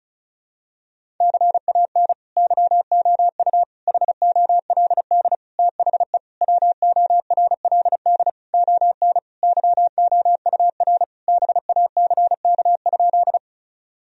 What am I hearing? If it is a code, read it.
Morse code, heard as a keyed tone, about 34 wpm, 700 Hz. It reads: CAN YOU HOLD THE WORLD ON YOUR BACK?
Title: Sentences from Top 1000 Words - Review Part 2 - 35wpm